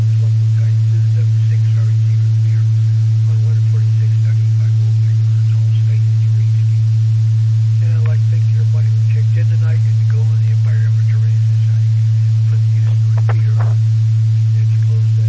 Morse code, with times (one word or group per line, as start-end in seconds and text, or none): none